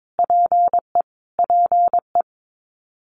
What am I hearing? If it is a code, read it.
Morse code, heard as a keyed tone, about 22 wpm, 700 Hz. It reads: PE PE
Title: Canadian Provinces and Territories - Repeated 3x - 22wpm